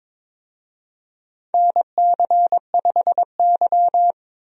Morse code, heard as a keyed tone, about 22 wpm, 700 Hz. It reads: NC5Y
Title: USA Call Signs - 4 Characters - 22wpm